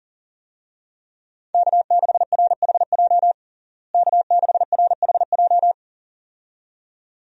Morse code, heard as a keyed tone, about 40 wpm, 700 Hz. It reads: K6RHJ K6RHJ
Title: USA Call Signs - 5 Characters - Repeated 2x - 40wpm